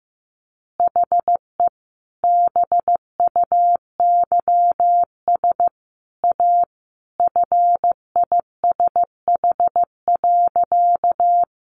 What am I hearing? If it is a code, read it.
Morse code, heard as a keyed tone, about 15 wpm, 700 Hz. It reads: HE BUYS A FISH.
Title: Easy Phrases 15wpm